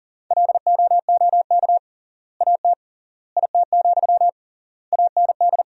Morse code, heard as a keyed tone, about 40 wpm, 700 Hz. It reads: LOOK AT IT, AND
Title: 3x Word-Spacing - Sentences from Top 200 Words - 40wpm